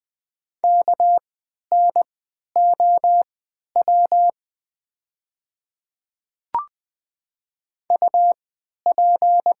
Morse code, heard as a keyed tone, about 20 wpm, spaced as for 11 wpm, 700 Hz. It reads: KNOW UP
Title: 3x Character Spacing - Top 200 Words - 20wpm